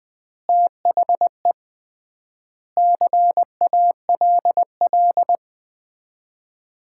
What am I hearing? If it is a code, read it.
Morse code, heard as a keyed tone, about 20 wpm, 700 Hz. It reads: THE CALL
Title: Common QSO Phrases - 3x Word-Spacing - 20wpm